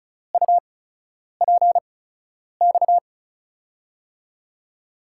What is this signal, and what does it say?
Morse code, heard as a keyed tone, about 36 wpm, spaced as for 10 wpm, 700 Hz. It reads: UPX
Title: USA Call Signs - 5 Characters - 8x Character Spacing - 35wpm